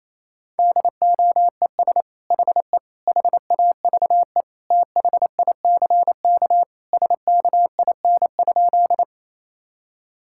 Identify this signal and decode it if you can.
Morse code, heard as a keyed tone, about 28 wpm, 700 Hz. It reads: DOES HE HAVE THICK SKIN?
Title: Sentences from Top 1600 Words 28wpm